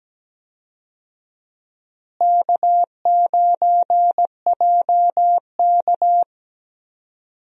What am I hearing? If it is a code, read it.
Morse code, heard as a keyed tone, about 17 wpm, 700 Hz. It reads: K9JK
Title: USA Call Signs - 4 Characters - Repeated 2x - 17wpm